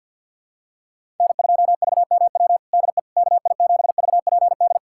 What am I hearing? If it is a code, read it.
Morse code, heard as a keyed tone, about 49 wpm, 700 Hz. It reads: N2VMW DE KI7VPD
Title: USA Callsign DE Callsign 50wpm